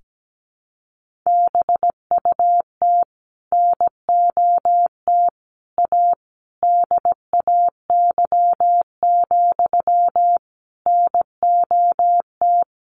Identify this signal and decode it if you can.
Morse code, heard as a keyed tone, about 17 wpm, 700 Hz. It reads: BUT NOT A DAY, NOT
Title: Sentences from Top 1400 Words 17wpm